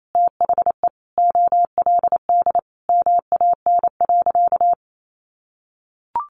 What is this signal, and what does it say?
Morse code, heard as a keyed tone, about 28 wpm, 700 Hz. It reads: THE OLD MAN.
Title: Sentences from Top 200 Words 28wpm